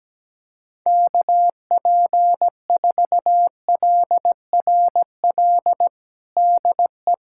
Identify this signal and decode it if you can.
Morse code, heard as a keyed tone, about 17 wpm, 700 Hz. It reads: KP4LRL DE
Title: USA Callsign DE Callsign 17wpm